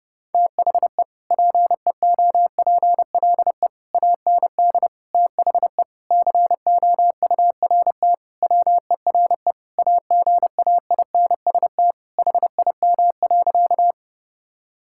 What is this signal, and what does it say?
Morse code, heard as a keyed tone, about 30 wpm, 700 Hz. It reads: THE PEOPLE AND THE COURT WERE AGAINST HIM.
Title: Sentences from Top 1600 Words 30wpm